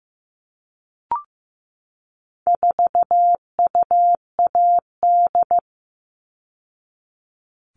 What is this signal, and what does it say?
Morse code, heard as a keyed tone, about 15 wpm, 700 Hz. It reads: 4UAD
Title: All Cut Numbers - 4-Digit 15wpm